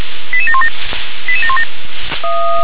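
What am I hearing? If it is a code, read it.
Five-tone selective call (ZVEI), heard as quick runs of five tones, tones of about 70 ms. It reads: 88717 88717
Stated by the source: ZVEI five-tone selective call received on SDR.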